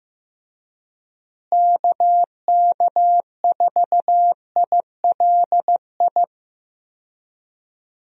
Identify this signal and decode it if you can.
Morse code, heard as a keyed tone, about 16 wpm, 700 Hz. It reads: KK4ILI
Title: USA Call Signs - 6 Characters - Repeated 3x - 15wpm